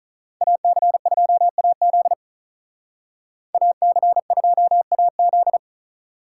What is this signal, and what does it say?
Morse code, heard as a keyed tone, about 44 wpm, 700 Hz. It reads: AC2AZ AC2AZ